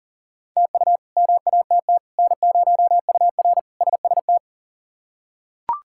Morse code, heard as a keyed tone, about 40 wpm, 700 Hz. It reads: TU MATT N0UR SST
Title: K1USN SST QSO Practice - 40wpm